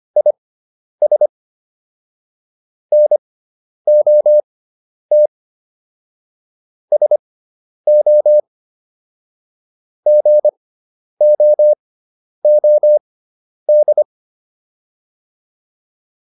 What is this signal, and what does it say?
Morse code, heard as a keyed tone, about 25 wpm, spaced as for 10 wpm, 600 Hz. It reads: IS NOT SO GOOD
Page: intermediate-practice-files